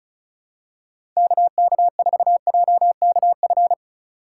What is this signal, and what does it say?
Morse code, heard as a keyed tone, about 36 wpm, 700 Hz. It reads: KK4JKF
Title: USA Call Signs - 6 Characters - 3x Character Spacing - 35wpm